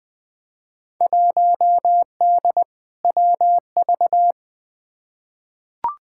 Morse code, heard as a keyed tone, about 20 wpm, 700 Hz. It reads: 1D WV
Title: Field Day Exchanges - 20wpm